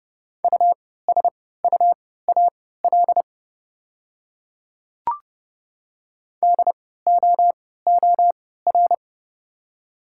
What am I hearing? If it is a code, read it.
Morse code, heard as a keyed tone, about 30 wpm, spaced as for 17 wpm, 700 Hz. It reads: USUAL DOOR
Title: 3x Character Spacing - Top 500 Words - 30wpm